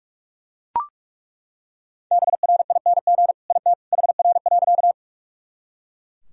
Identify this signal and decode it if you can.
Morse code, heard as a keyed tone, about 41 wpm, 700 Hz. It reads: BRING IT HR.
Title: Sentences from Top 500 Words and Abbreviations 45wpm